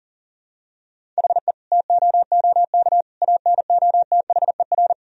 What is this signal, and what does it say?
Morse code, heard as a keyed tone, about 40 wpm, 700 Hz. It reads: HE TOOK ANOTHER
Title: Sentences from Top 1000 Words - Review Part 2 - 40wpm